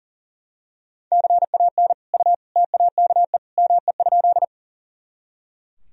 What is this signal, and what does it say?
Morse code, heard as a keyed tone, about 40 wpm, 700 Hz. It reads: CAN U TAKE ME?
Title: Sentences from Top 200 Words and Abbreviations 40wpm